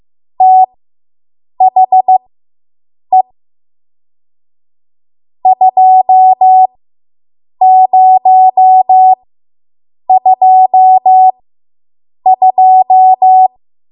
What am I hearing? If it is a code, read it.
Morse code, heard as a keyed tone, about 15 wpm, spaced as for 7 wpm, 750 Hz. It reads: THE 2022